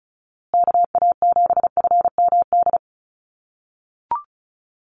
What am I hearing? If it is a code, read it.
Morse code, heard as a keyed tone, about 34 wpm, 700 Hz. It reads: KA7FMD
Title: USA Call Signs Repeated 2x 35wpm